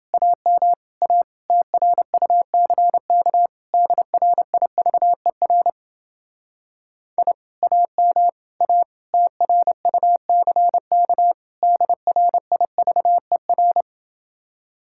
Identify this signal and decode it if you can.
Morse code, heard as a keyed tone, about 31 wpm, 700 Hz. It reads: AM A TRUCK DRIVER I AM A TRUCK DRIVER